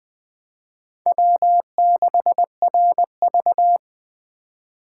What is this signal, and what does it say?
Morse code, heard as a keyed tone, about 20 wpm, 700 Hz. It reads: W6RV
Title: USA Call Signs - 4 Characters - 4x Character Spacing - 20wpm